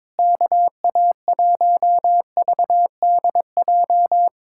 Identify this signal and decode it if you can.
Morse code, heard as a keyed tone, about 21 wpm, 700 Hz. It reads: KA1VDJ
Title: USA Callsign DE Callsign - 6 Characters - 22wpm